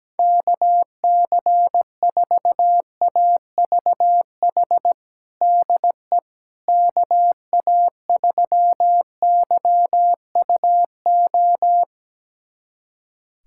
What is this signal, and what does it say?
Morse code, heard as a keyed tone, about 17 wpm, 700 Hz. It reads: KC4AVH DE KA3YUO
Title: USA Callsign DE Callsign - 6 Characters - 17wpm